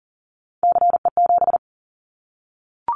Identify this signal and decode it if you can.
Morse code, heard as a keyed tone, about 40 wpm, 700 Hz. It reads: CE7
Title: Callsign Prefixes for Top 100 DXCC - 40wpm